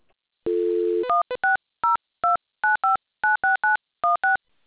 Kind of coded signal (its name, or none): DTMF